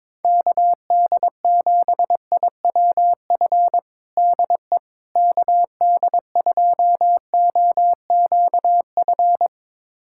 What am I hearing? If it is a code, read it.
Morse code, heard as a keyed tone, about 22 wpm, 700 Hz. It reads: KD7IWF DE KD2OQF
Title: USA Callsign DE Callsign - 6 Characters - 22wpm